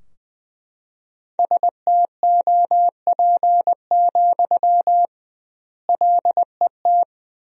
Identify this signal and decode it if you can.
Morse code, heard as a keyed tone, about 20 wpm, 700 Hz. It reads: STOP, LET
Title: 2x Word-Spacing - Sentences from Top 300 Words - 20wpm